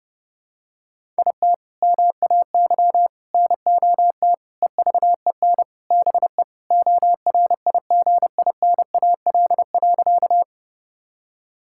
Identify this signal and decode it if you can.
Morse code, heard as a keyed tone, about 30 wpm, 700 Hz. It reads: IT MAY NOT EVEN BE ORIGINAL.
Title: Sentences from Top 1000 Words - Review Part 6 - 30wpm